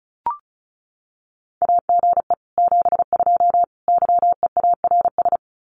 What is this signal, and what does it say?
Morse code, heard as a keyed tone, about 34 wpm, 700 Hz. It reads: AGE 72 YEARS
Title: Common QSO Phrases Encore 35wpm